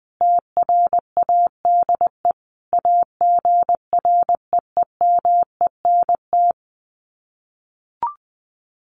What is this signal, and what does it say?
Morse code, heard as a keyed tone, about 20 wpm, 700 Hz. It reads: TRADE AGREEMENT
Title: Sets of 5 Words Encore 20wpm